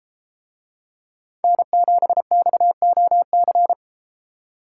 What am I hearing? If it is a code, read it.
Morse code, heard as a keyed tone, about 33 wpm, 700 Hz. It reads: N7XOC